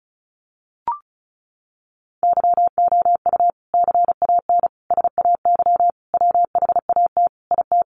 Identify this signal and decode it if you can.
Morse code, heard as a keyed tone, about 34 wpm, 700 Hz. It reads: YOU CAN SAY WHAT IT
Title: Sentences from Top 200 Words 35wpm